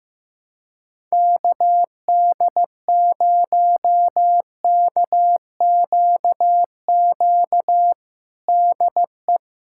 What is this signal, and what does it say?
Morse code, heard as a keyed tone, about 15 wpm, 700 Hz. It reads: KD0KQQ DE
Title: USA Callsign DE Callsign - 6 Characters - 15wpm